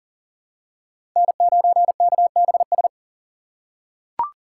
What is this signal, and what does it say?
Morse code, heard as a keyed tone, about 40 wpm, 700 Hz. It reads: N9KBS
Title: USA Call Signs - 5 Characters - 6x Character Spacing - 40wpm